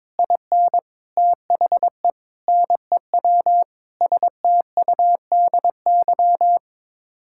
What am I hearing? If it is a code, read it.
Morse code, heard as a keyed tone, about 22 wpm, 700 Hz. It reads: IN THE NEW STUDY
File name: practice / 22wpm / Sets of 8 Words - 22wpm